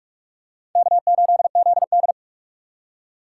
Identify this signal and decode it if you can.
Morse code, heard as a keyed tone, about 45 wpm, 700 Hz. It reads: K8ZD
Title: USA Call Signs - 3x Character Spacing - 45wpm